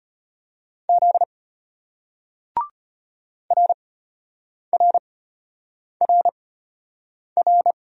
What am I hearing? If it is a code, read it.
Morse code, heard as a keyed tone, about 35 wpm, spaced as for 8 wpm, 700 Hz. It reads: Z RRRR